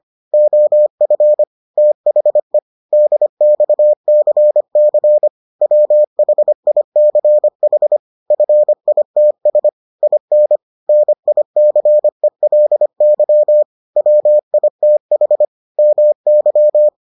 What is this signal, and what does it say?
Morse code, heard as a keyed tone, about 25 wpm, 600 Hz. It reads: OF THE DXCC WHICH FITS IN NICELY WITH MY